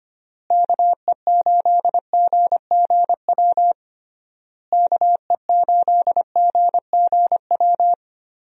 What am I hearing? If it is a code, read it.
Morse code, heard as a keyed tone, about 25 wpm, 700 Hz. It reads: KE8GGW KE8GGW
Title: USA Call Signs - 6 Characters - Repeated 3x - 25wpm